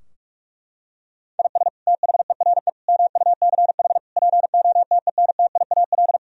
Morse code, heard as a keyed tone, about 44 wpm, 700 Hz. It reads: IS THERE MUCH POTENTIAL
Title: Sentences from Top 1000 Words - Review Part 4 - 45wpm